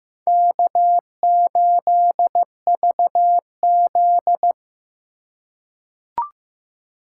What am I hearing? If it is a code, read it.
Morse code, heard as a keyed tone, about 15 wpm, 700 Hz. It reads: K8VZ
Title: USA Call Signs - 4 Characters - 2x Character Spacing - 15wpm